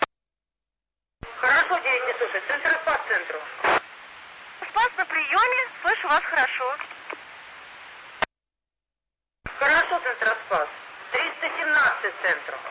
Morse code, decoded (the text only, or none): none